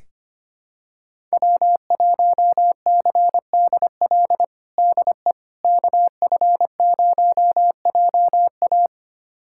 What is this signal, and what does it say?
Morse code, heard as a keyed tone, about 25 wpm, 700 Hz. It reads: W1CDL DE KF0JA
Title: USA Callsign DE Callsign - 5 Characters - 25wpm